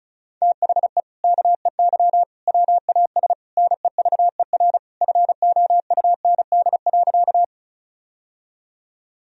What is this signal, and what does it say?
Morse code, heard as a keyed tone, about 34 wpm, 700 Hz. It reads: THE KEY WAS NEVER FOUND.
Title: Sentences from Top 1000 Words - Review Part 4 - 35wpm